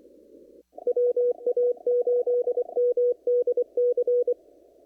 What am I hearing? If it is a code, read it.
Morse code, heard as a keyed tone, about 24 wpm, 470 Hz. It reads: WA8MDC
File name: CW / kt5j wpx cw-op-k5pi